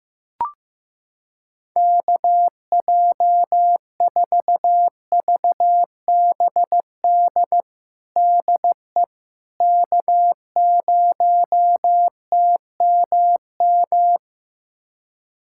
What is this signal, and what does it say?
Morse code, heard as a keyed tone, about 15 wpm, 700 Hz. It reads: KJ4VBD DE K0TMM
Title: USA Callsign DE Callsign 15wpm